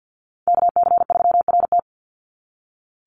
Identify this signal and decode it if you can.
Morse code, heard as a keyed tone, about 48 wpm, 700 Hz. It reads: KC3RT
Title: NA SOTA Chaser Callsigns 50wpm